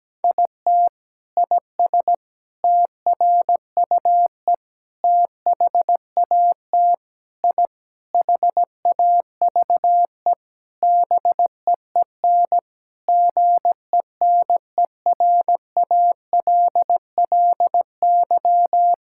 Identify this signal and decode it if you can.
Morse code, heard as a keyed tone, about 17 wpm, 700 Hz. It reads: IT IS TRUE THAT I HAVE BEEN GENERALLY
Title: Sentences from Top 1000 Words - Review Part 6 - 17wpm